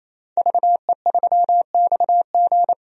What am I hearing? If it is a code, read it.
Morse code, heard as a keyed tone, about 28 wpm, 700 Hz. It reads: VE3XG
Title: DXCC-100 Call Signs - 5 Characters - 28wpm